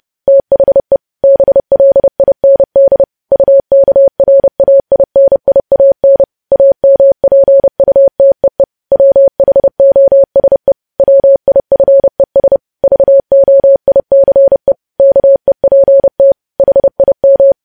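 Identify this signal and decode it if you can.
Morse code, heard as a keyed tone, about 30 wpm, 550 Hz. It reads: THE BLIND UKRAINIAN AMPUTEE WHOSE WIFES VOICE KEPT HIM
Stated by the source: News headlines sent at 30 wpm.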